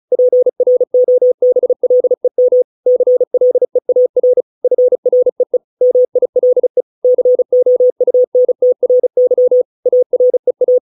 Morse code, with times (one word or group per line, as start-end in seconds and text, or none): 0.1-2.6 PROBLEM
2.9-4.4 CLEAR
4.6-5.6 FREE
5.8-6.8 MILE
7.0-9.6 COUNTRY
9.9-10.8 AREA